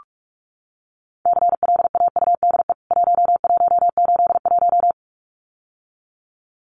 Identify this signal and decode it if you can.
Morse code, heard as a keyed tone, about 44 wpm, 700 Hz. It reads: CLAUDE 1181
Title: CWT Exchange 45wpm